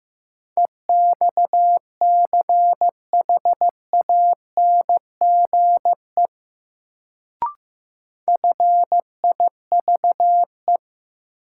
Morse code, heard as a keyed tone, about 15 wpm, 700 Hz. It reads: EXCHANGE FIVE
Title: Sets of 5 Words - New Word Drill - 15wpm